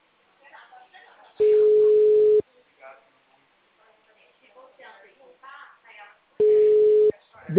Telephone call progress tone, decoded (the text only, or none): ringback tone